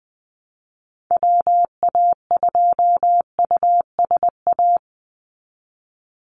Morse code, heard as a keyed tone, about 20 wpm, 700 Hz. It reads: WA2USA
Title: NA SOTA Chaser Callsigns 20wpm